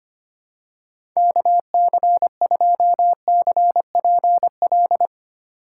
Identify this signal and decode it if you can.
Morse code, heard as a keyed tone, about 25 wpm, 700 Hz. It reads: KC2CPL
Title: USA Call Signs - 6 Characters - 3x Character Spacing - 25wpm